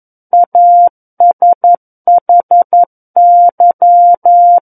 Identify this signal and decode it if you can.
Morse code, heard as a keyed tone, about 11 wpm, 700 Hz. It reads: ASHY
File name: 11 WPM